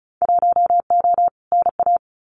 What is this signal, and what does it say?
Morse code, heard as a keyed tone, about 36 wpm, 700 Hz. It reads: 1O NA
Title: Winter Field Day Exchanges 35wpm